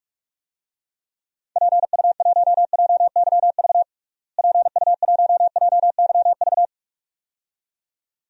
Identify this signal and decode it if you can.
Morse code, heard as a keyed tone, about 41 wpm, 700 Hz. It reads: PU1JYV PU1JYV
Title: DXCC-100 Call Signs - 6 Characters - Repeated 2x - 45wpm